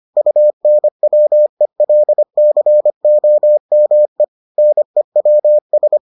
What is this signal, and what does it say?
Morse code, heard as a keyed tone, about 25 wpm, 600 Hz. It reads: UNWELCOME NEWS